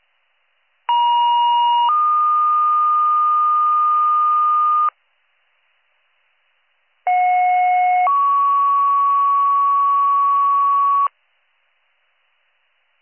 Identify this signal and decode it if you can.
Two-tone page, heard as a one-second tone, then a three-second tone, about 950 then 1220 Hz, about 730 then 1120 Hz. It is heard from 0.9 to 11.1 s.